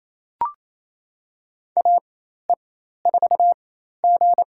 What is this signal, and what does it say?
Morse code, heard as a keyed tone, about 28 wpm, spaced as for 13 wpm, 700 Hz. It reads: AE4G
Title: USA Call Signs - 4 Characters - 4x Character Spacing - 28wpm